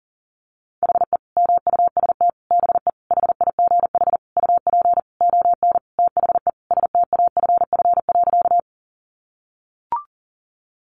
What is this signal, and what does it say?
Morse code, heard as a keyed tone, about 40 wpm, 700 Hz. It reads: HE MUST BE HIGH UP ON THE STAFF.
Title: Sentences from Top 1200 Words 40wpm